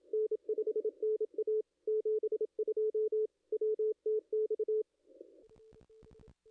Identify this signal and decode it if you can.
Morse code, heard as a keyed tone, about 27 wpm, 430 Hz. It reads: N5NA 72 WTX
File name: contests / ss-cw-k5tr-op-n5rz -0200